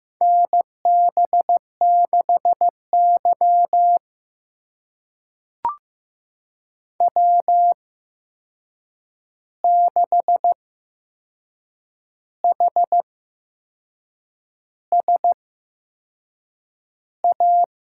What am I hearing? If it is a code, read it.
Morse code, heard as a keyed tone, about 15 wpm, spaced as for 6 wpm, 700 Hz. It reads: NB6Y W6HSA
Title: USA Call Signs - 8x Character Spacing - 15wpm